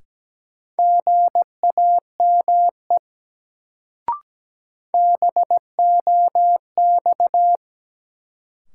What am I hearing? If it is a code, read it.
Morse code, heard as a keyed tone, about 17 wpm, 700 Hz. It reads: GAME BOX
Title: Sentences from Top 500 Words - New Word Drill - 17wpm